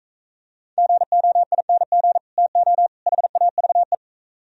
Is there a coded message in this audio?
Yes: Morse code from 0.8 to 4.0 s, heard as a keyed tone, about 43 wpm, 700 Hz. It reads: GOING TO HAVE